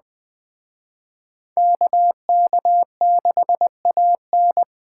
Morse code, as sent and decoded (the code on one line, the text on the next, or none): -.- -.- -.... .- -.
KK6AN